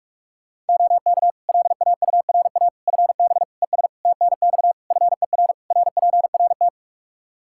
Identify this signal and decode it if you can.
Morse code, heard as a keyed tone, about 43 wpm, 700 Hz. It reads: OK LAURA FB ES TNX FER RPRT
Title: K1USN SST QSO Practice 45wpm